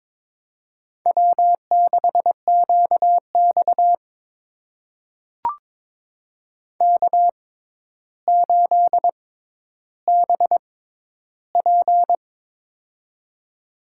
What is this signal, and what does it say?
Morse code, heard as a keyed tone, about 22 wpm, spaced as for 11 wpm, 700 Hz. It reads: W6QX K8BP
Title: USA Call Signs - 4 Characters - 6x Character Spacing - 22wpm